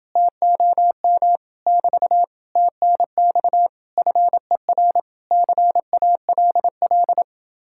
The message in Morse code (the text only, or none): TOM <BT> TNX FER CALL